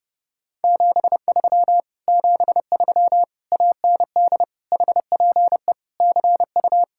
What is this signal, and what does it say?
Morse code, heard as a keyed tone, about 33 wpm, 690 Hz. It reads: 73 73 AND HPE CU